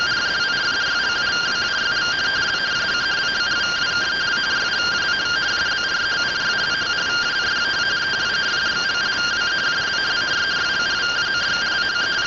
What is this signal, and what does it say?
Two-tone teleprinter signal, about 170 Hz shift.